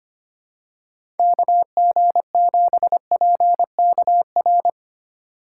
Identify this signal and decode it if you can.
Morse code, heard as a keyed tone, about 25 wpm, 700 Hz. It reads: KG7PKR
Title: USA Call Signs - 6 Characters - 4x Character Spacing - 25wpm